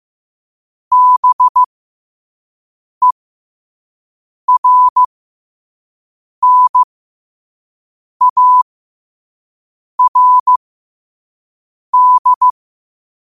Morse code, heard as a keyed tone, about 15 wpm, spaced as for 5 wpm, 1000 Hz. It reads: BERNARD
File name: qotdfiles / 5wpm